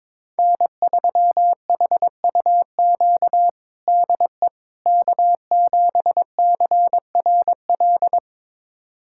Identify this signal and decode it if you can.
Morse code, heard as a keyed tone, about 22 wpm, 700 Hz. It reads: N3HUQ DE K7CRL
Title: USA Callsign DE Callsign - 5 Characters - 22wpm